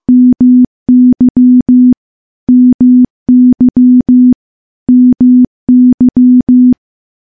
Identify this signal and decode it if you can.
Morse code, heard as a keyed tone, about 14 wpm, 250 Hz. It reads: MY MY MY
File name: morse / morse-training-15WPM